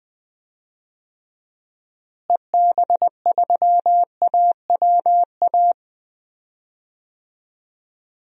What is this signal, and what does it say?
Morse code, heard as a keyed tone, about 20 wpm, 700 Hz. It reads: EB3AWA
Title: DXCC-100 Call Signs - 6 Characters - Repeated 2x - 20wpm